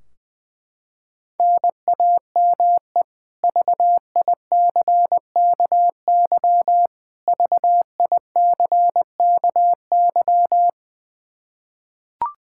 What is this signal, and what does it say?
Morse code, heard as a keyed tone, about 21 wpm, 700 Hz. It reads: NAME VICKY VICKY
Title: K1USN SST QSO Practice 20wpm